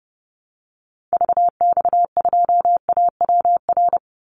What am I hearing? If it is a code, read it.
Morse code, heard as a keyed tone, about 30 wpm, 700 Hz. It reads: VX2AWR
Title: Call Signs Repeated 2x - 30wpm